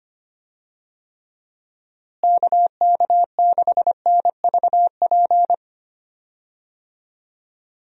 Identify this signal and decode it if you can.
Morse code, heard as a keyed tone, about 25 wpm, 700 Hz. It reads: KK6NVP
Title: USA Call Signs - 6 Characters - Repeated 3x - 25wpm